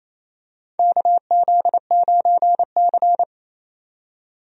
Morse code, heard as a keyed tone, about 28 wpm, 700 Hz. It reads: KZ9C
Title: USA Call Signs - 4 Characters - 28wpm